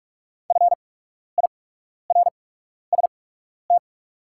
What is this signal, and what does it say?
Morse code, heard as a keyed tone, about 44 wpm, spaced as for 12 wpm, 700 Hz. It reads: FIRST
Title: 8x Character Spacing - Top 100 Words - 45wpm